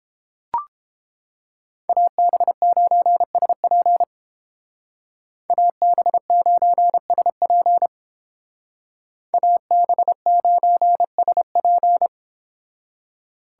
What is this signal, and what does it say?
Morse code, heard as a keyed tone, about 29 wpm, 700 Hz. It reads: AB9SP AB9SP AB9SP